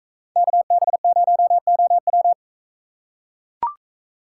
Morse code, heard as a keyed tone, about 44 wpm, 700 Hz. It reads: KB0OW